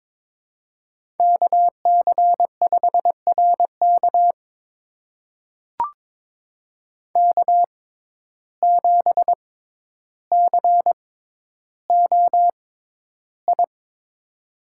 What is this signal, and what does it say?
Morse code, heard as a keyed tone, about 22 wpm, spaced as for 11 wpm, 700 Hz. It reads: KC5RK K7COI
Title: USA Call Signs - 5 Characters - 6x Character Spacing - 22wpm